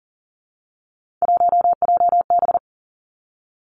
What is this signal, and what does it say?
Morse code, heard as a keyed tone, about 38 wpm, 700 Hz. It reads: 1JB
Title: Three Challenging Letter-Number Combinations 40wpm